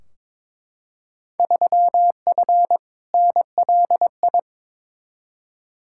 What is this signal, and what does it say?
Morse code, heard as a keyed tone, about 22 wpm, 700 Hz. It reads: 3F NLI
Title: Field Day Exchanges Encore - 22wpm